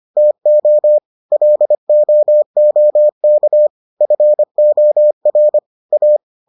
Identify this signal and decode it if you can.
Morse code, heard as a keyed tone, about 25 wpm, 600 Hz. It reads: TO LOOK FOR A